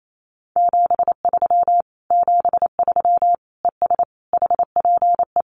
Morse code, heard as a keyed tone, about 28 wpm, 700 Hz. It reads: 73 73 ES HPE
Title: Common QSO Phrases Encore 28wpm